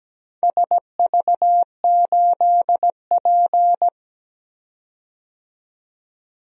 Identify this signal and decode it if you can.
Morse code, heard as a keyed tone, about 17 wpm, 700 Hz. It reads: SV8P
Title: DXCC-100 Call Signs - 4 Characters - 17wpm